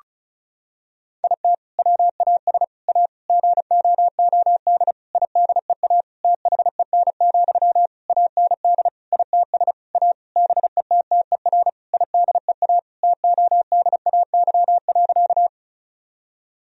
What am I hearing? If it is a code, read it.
Morse code, heard as a keyed tone, about 36 wpm, 700 Hz. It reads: IT WAS A GOOD IDEA THEN, AND ITS A BETTER IDEA TODAY.
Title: Sentences from Top 1000 Words - Review Part 4 - 35wpm